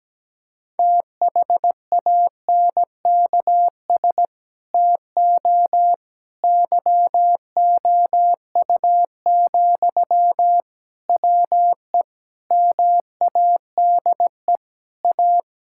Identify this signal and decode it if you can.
Morse code, heard as a keyed tone, about 17 wpm, 700 Hz. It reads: THANKS TO YOU, WE MADE A
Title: Sentences from Top 1000 Words - Review Part 5 - 17wpm